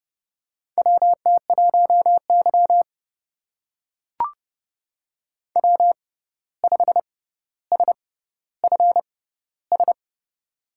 Morse code, heard as a keyed tone, about 31 wpm, spaced as for 14 wpm, 700 Hz. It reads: WT1Y W5SFS
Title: USA Call Signs - 6x Character Spacing - 30wpm